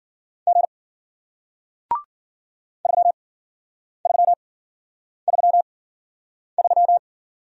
Morse code, heard as a keyed tone, about 53 wpm, spaced as for 9 wpm, 700 Hz. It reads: G 3333